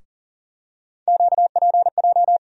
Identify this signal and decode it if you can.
Morse code, heard as a keyed tone, about 39 wpm, 700 Hz. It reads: QPJ
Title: USA Call Sign Postfixes - 3 Letters 40wpm